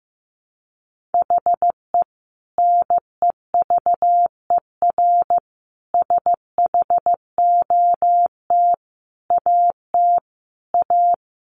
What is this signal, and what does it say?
Morse code, heard as a keyed tone, about 14 wpm, 700 Hz. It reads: HE NEVER SHOT AT A
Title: Sentences from Top 1100 Words 15wpm